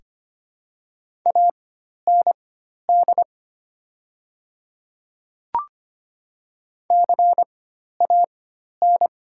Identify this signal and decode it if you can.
Morse code, heard as a keyed tone, about 25 wpm, spaced as for 12 wpm, 700 Hz. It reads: AND CAN
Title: 4x Character Spacing - Top 300 Words - 25wpm